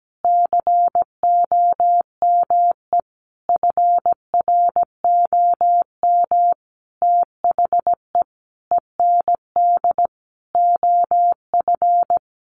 Morse code, heard as a keyed tone, about 17 wpm, 700 Hz. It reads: COME FROM THE END OF
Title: Sentences from Top 200 Words 17wpm